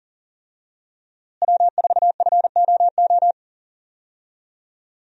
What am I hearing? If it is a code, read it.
Morse code, heard as a keyed tone, about 38 wpm, 690 Hz. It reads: W4FOO
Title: USA Call Signs - 5 Characters - Repeated 3x - 40wpm